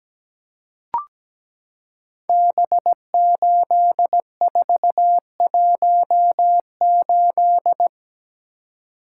Morse code, heard as a keyed tone, about 17 wpm, 700 Hz. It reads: B8418
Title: Code Groups - 5 Letter:Numbers - 17wpm